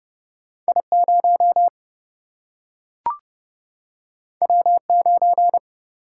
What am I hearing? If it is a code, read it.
Morse code, heard as a keyed tone, about 30 wpm, 700 Hz. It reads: I0 W9
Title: Single-Letter Single-Number 30wpm